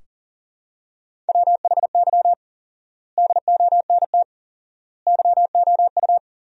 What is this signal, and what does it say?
Morse code, heard as a keyed tone, about 40 wpm, 700 Hz. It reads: WHY DONT YOU
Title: 4x Word-Spacing - Sentences from Top 300 Words - 40wpm